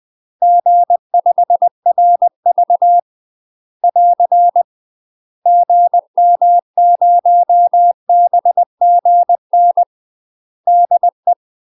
Morse code, heard as a keyed tone, about 20 wpm, 700 Hz. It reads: G5RV <AR> GM0BGN DE
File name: UK MASTER QSOs TRAINING/20WPM